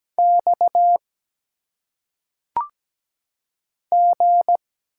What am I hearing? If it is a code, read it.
Morse code, heard as a keyed tone, about 17 wpm, 700 Hz. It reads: X G
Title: Single Letters - Phonetic Pronunciation 17wpm